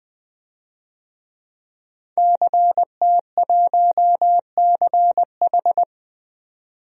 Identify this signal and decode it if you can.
Morse code, heard as a keyed tone, about 20 wpm, 700 Hz. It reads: CT1CH
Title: DXCC-100 Call Signs - 5 Characters - 20wpm